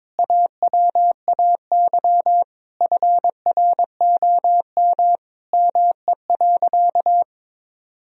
Morse code, heard as a keyed tone, about 22 wpm, 700 Hz. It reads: AWAY FROM ME.
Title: Sentences from Top 1000 Words - Review Part 1 - 22wpm